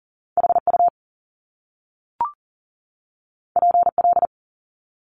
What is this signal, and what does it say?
Morse code, heard as a keyed tone, about 41 wpm, 700 Hz. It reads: HU PL
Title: Top Two Letter Combinations in Words 40wpm